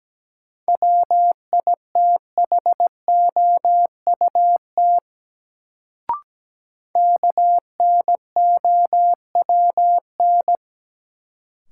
Morse code, heard as a keyed tone, about 17 wpm, 700 Hz. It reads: WITHOUT KNOWN
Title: Sentences from Top 600 Words - New Word Drill - 17wpm